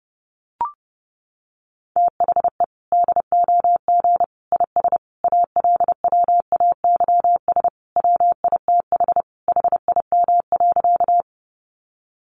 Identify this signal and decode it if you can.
Morse code, heard as a keyed tone, about 30 wpm, 700 Hz. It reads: THE DOG IS ALWAYS WITH HIM.
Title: Sentences from Top 400 Words 30wpm